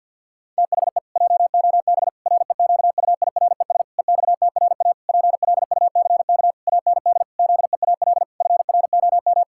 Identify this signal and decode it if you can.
Morse code, heard as a keyed tone, about 50 wpm, 700 Hz. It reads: THE JOB REQUIRES EXTRA PLUCK AND ZEAL FROM
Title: Pangram Challenge v2 - 50wpm